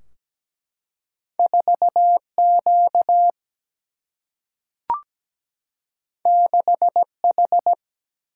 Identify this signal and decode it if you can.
Morse code, heard as a keyed tone, about 17 wpm, 700 Hz. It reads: 4Q 6H
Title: Single-Letter Single-Number 17wpm